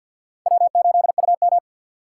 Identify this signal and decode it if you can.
Morse code, heard as a keyed tone, about 50 wpm, 690 Hz. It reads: W8UM
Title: USA Call Signs - 2x Character Spacing - 50wpm